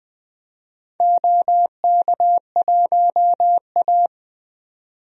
Morse code, heard as a keyed tone, about 20 wpm, 700 Hz. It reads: OK1A
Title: DXCC-100 Call Signs - 4 Characters - 20wpm